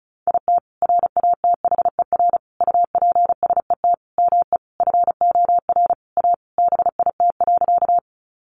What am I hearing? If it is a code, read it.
Morse code, heard as a keyed tone, about 34 wpm, 700 Hz. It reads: IT RATHER UPSET ME FOR A BIT.
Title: Sentences from Top 900 Words 35wpm